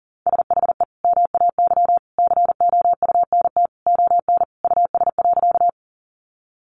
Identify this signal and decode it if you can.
Morse code, heard as a keyed tone, about 40 wpm, 700 Hz. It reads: SHE MAY COUNT ON US.
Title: Sentences from Top 1500 Words 40wpm